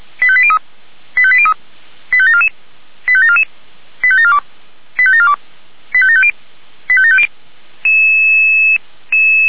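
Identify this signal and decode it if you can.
Five-tone selective call (ZVEI), heard as quick runs of five tones, tones of about 70 ms. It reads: 85803 85803 85744 85744 85742 85742 85750 85750
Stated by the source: ZVEI five-tone selective call 89485 received by SDR, Feuerwehr Orlamünde.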